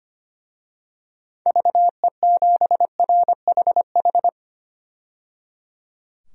Morse code, heard as a keyed tone, about 25 wpm, 690 Hz. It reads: VE7RHH
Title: DXCC-100 Call Signs - 6 Characters - 25wpm